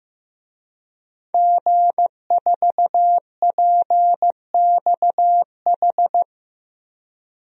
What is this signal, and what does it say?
Morse code, heard as a keyed tone, about 15 wpm, 700 Hz. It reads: G4PXH
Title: DXCC-100 Call Signs - 5 Characters - 15wpm